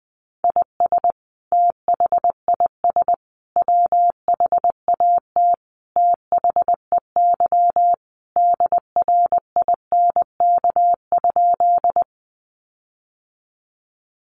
Morse code, heard as a keyed tone, about 20 wpm, 700 Hz. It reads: IS THIS WHAT THEY DRINK?
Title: Sentences from Top 1500 Words 20wpm